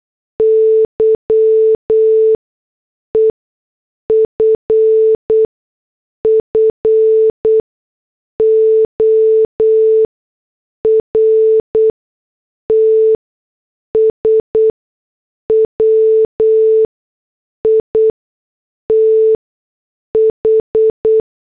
Morse code, heard as a keyed tone, about 8 wpm, spaced as for 6 wpm, 440 Hz. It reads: Y EFFORTS WITH
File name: morse / headlines